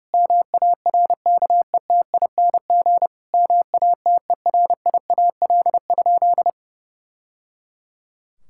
Morse code, heard as a keyed tone, about 32 wpm, 700 Hz. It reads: MARKETING MATERIAL?
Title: Sentences from Top 1000 Words - Review Part 5 - 30wpm